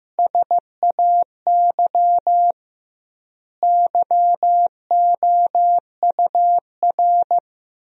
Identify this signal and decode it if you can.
Morse code, heard as a keyed tone, about 16 wpm, 700 Hz. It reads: SAY YOUR
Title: 2x Word-Spacing - Sentences from Top 200 Words - 15wpm